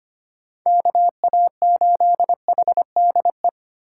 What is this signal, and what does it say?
Morse code, heard as a keyed tone, about 25 wpm, 700 Hz. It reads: KA8HDE
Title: K1USN SST QSO Practice - 25wpm